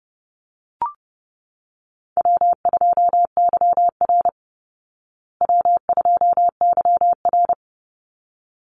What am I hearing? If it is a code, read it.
Morse code, heard as a keyed tone, about 30 wpm, 700 Hz. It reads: W2YR W2YR
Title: Call Signs Repeated 3x - 30wpm